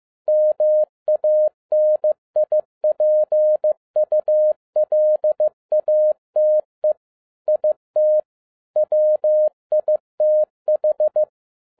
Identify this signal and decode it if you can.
Morse code, heard as a keyed tone, about 15 wpm, 610 Hz. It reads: MANIPULATE IT WITH